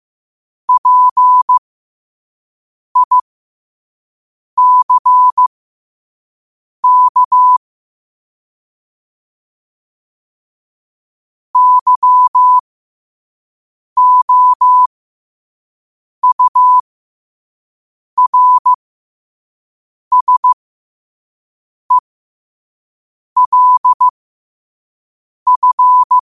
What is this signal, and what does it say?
Morse code, heard as a keyed tone, about 15 wpm, spaced as for 5 wpm, 1000 Hz. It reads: PICK YOURSELF
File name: qotdfiles / 5wpm